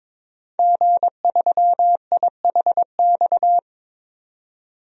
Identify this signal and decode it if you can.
Morse code, heard as a keyed tone, about 21 wpm, 700 Hz. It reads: G3IHX
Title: DXCC-100 Call Signs - 5 Characters - 22wpm